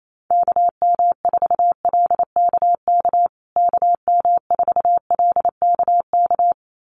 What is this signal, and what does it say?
Morse code, heard as a keyed tone, about 28 wpm, 700 Hz. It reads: KM4LKK KM4LKK
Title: USA Call Signs Repeated 2x 28wpm